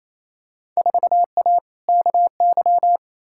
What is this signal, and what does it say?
Morse code, heard as a keyed tone, about 28 wpm, 700 Hz. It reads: 4A KY
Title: Field Day Exchanges Encore - 28wpm